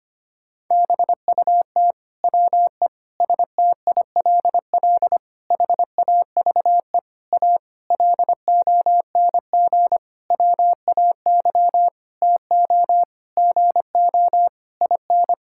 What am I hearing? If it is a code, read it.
Morse code, heard as a keyed tone, about 25 wpm, 700 Hz. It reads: BUT WE STILL HAVE A LONG WAY TO GO IN
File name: practice / 25wpm / Sentences with 9-Word Phrases - 25wpm